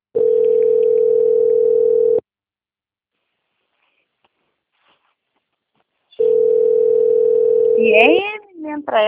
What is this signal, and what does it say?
Ringback tone (North America), 440 + 480 Hz, with speech.
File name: out- - - - - .3061